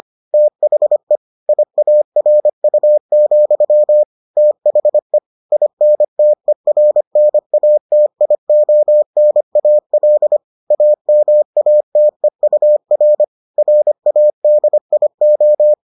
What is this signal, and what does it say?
Morse code, heard as a keyed tone, about 25 wpm, 600 Hz. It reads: THE IARU, THE INTERNATIONAL AMATEUR RADIO